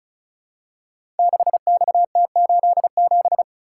Morse code, heard as a keyed tone, about 36 wpm, 690 Hz. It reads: 6XT87
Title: Code Groups - 5 Letter:Numbers - 35wpm